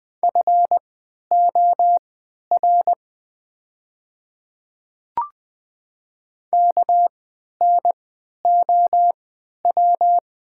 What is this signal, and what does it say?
Morse code, heard as a keyed tone, about 20 wpm, spaced as for 11 wpm, 700 Hz. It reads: FOR KNOW
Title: 3x Character Spacing - Top 100 Words - 20wpm